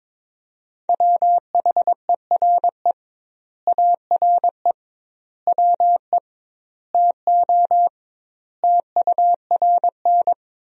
Morse code, heard as a keyed tone, about 22 wpm, 700 Hz. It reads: WHERE ARE WE TO TURN
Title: 2x Word-Spacing - Sentences from Top 200 Words - 22wpm